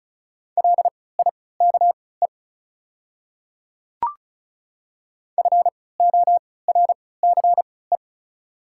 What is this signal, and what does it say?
Morse code, heard as a keyed tone, about 34 wpm, spaced as for 20 wpm, 690 Hz. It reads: LIKE FORCE
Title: 3x Character Spacing - Top 500 Words - 35wpm